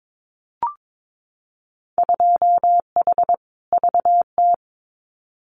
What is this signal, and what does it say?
Morse code, heard as a keyed tone, about 22 wpm, 700 Hz. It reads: 2H VT
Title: Winter Field Day Exchanges 22wpm